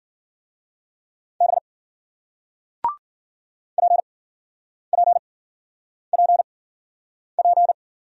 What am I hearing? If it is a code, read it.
Morse code, heard as a keyed tone, about 54 wpm, spaced as for 9 wpm, 700 Hz. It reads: B PPPP